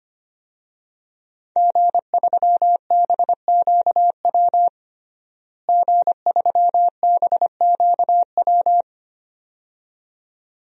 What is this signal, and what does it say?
Morse code, heard as a keyed tone, about 25 wpm, 700 Hz. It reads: G3BQW G3BQW
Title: DXCC-100 Call Signs - 5 Characters - Repeated 2x - 25wpm